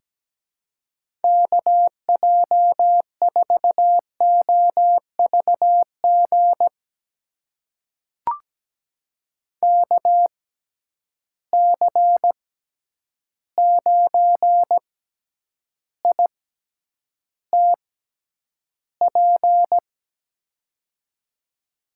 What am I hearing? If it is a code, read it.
Morse code, heard as a keyed tone, about 17 wpm, spaced as for 9 wpm, 700 Hz. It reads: KJ4OVG KC9ITP
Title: USA Call Signs - 6 Characters - 6x Character Spacing - 17wpm